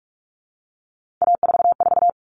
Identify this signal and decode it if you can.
Morse code, heard as a keyed tone, about 44 wpm, 700 Hz. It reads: A44
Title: Common Cut Numbers - 3-Digit 45wpm